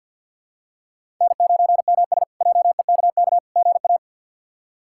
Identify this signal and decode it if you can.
Morse code, heard as a keyed tone, about 51 wpm, 700 Hz. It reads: N9MS JEKK GA
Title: K1USN SST QSO Practice - 50wpm